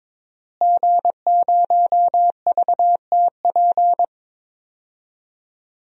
Morse code, heard as a keyed tone, about 22 wpm, 700 Hz. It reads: G0VTP
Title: DXCC-100 Call Signs - 5 Characters - Repeated 3x - 22wpm